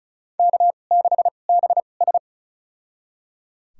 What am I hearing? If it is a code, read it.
Morse code, heard as a keyed tone, about 36 wpm, spaced as for 25 wpm, 700 Hz. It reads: K6BS
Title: USA Call Signs - 4 Characters - 2x Character Spacing - 35wpm